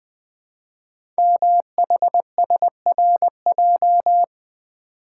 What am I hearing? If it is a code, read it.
Morse code, heard as a keyed tone, about 20 wpm, 700 Hz. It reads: MHSRJ
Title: Code Groups - 5 Letters - 20wpm